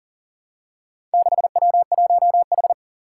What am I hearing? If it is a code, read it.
Morse code, heard as a keyed tone, about 40 wpm, 700 Hz. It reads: 6W1H
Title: Four Challenging Letter-Number Combinations 40wpm